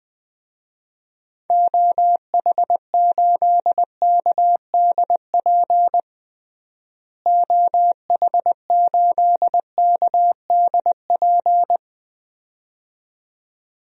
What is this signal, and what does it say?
Morse code, heard as a keyed tone, about 20 wpm, 700 Hz. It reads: OH8KDP OH8KDP
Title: DXCC-100 Call Signs - 6 Characters - Repeated 2x - 20wpm